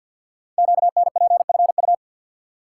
Morse code, heard as a keyed tone, about 51 wpm, 700 Hz. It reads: QNPFU
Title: Code Groups - 5 Letters - 50wpm